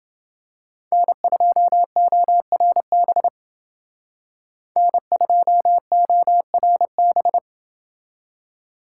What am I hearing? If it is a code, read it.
Morse code, heard as a keyed tone, about 28 wpm, 700 Hz. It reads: N2ORB N2ORB